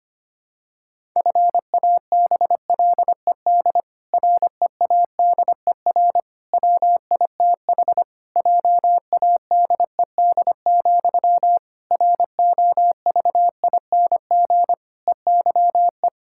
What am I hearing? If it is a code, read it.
Morse code, heard as a keyed tone, about 25 wpm, 700 Hz. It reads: FABLED READER WITH JADED, ROVING EYE
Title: Pangram Challenge v2 - 25wpm